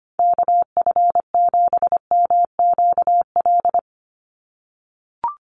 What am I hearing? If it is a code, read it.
Morse code, heard as a keyed tone, about 25 wpm, 700 Hz. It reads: KF7MQL
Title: USA Call Signs 25wpm